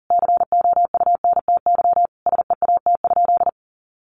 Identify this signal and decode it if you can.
Morse code, heard as a keyed tone, about 40 wpm, 700 Hz. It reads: COUNTY SEAT?
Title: Sentences from Top 1500 Words 40wpm